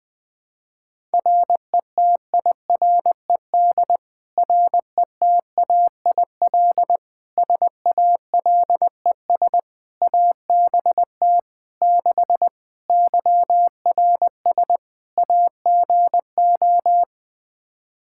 Morse code, heard as a keyed tone, about 21 wpm, 700 Hz. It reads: RETIRED RETAIL SALES ABT 6 YRS AGO
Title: Practice QSO - Exchange 3 - More Info - 20wpm